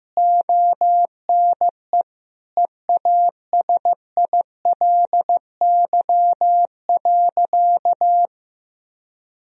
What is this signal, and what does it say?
Morse code, heard as a keyed tone, about 16 wpm, 700 Hz. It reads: ONE EASILY.
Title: Sentences from Top 1000 Words - Review Part 3 - 15wpm